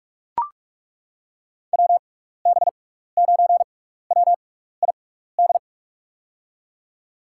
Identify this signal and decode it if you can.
Morse code, heard as a keyed tone, about 42 wpm, spaced as for 15 wpm, 700 Hz. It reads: WB9WID
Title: USA Call Signs - 6x Character Spacing - 45wpm